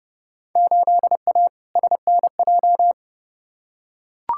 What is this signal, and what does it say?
Morse code, heard as a keyed tone, about 29 wpm, 700 Hz. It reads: 8A SNJ
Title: Field Day Exchanges Encore - 30wpm